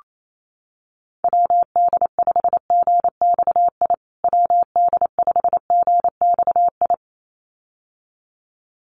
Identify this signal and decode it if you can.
Morse code, heard as a keyed tone, about 28 wpm, 700 Hz. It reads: WD5GXI WD5GXI
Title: USA Call Signs Repeated 2x 28wpm